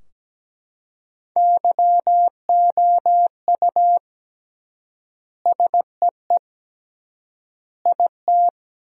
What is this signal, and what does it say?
Morse code, heard as a keyed tone, about 17 wpm, 700 Hz. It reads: YOU SEE IT
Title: 3x Word-Spacing - Sentences from Top 200 Words - 17wpm